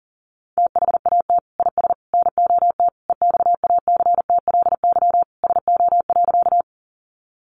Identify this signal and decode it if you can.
Morse code, heard as a keyed tone, about 40 wpm, 700 Hz. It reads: THAT IS NOT EXACTLY SO.
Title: Sentences from Top 900 Words 40wpm